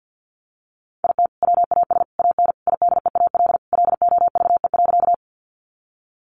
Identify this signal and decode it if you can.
Morse code, heard as a keyed tone, about 50 wpm, 700 Hz. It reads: IT WAS AN IDEAL LOVE.
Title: Sentences from Top 1400 Words 50wpm